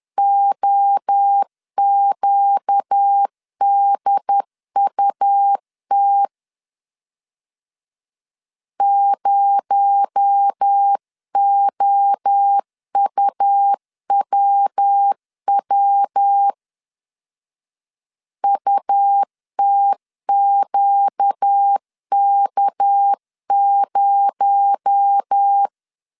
Morse code, heard as a keyed tone, about 10 wpm, 800 Hz. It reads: OQDUT 0OUWW UTQK0